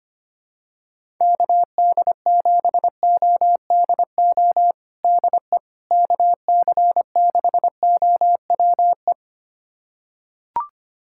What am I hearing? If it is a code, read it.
Morse code, heard as a keyed tone, about 25 wpm, 700 Hz. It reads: KD7ODO DE KC6OWE
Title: USA Callsign DE Callsign - 6 Characters - 25wpm